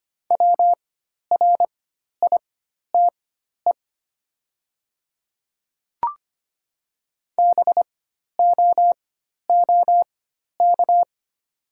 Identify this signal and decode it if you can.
Morse code, heard as a keyed tone, about 25 wpm, spaced as for 12 wpm, 700 Hz. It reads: WRITE BOOK
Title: 4x Character Spacing - Top 300 Words - 25wpm